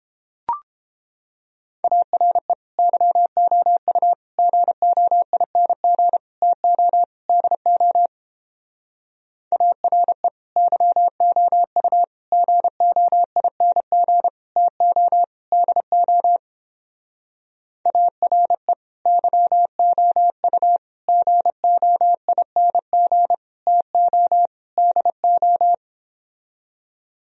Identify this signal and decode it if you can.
Morse code, heard as a keyed tone, about 31 wpm, 700 Hz. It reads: ARE YOU GOING TO DO ARE YOU GOING TO DO ARE YOU GOING TO DO